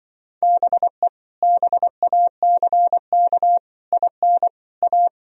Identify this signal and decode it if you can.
Morse code, heard as a keyed tone, about 24 wpm, 700 Hz. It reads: BE BACK IN A
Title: Speed-Racing - Sets of 5 Words 20wpm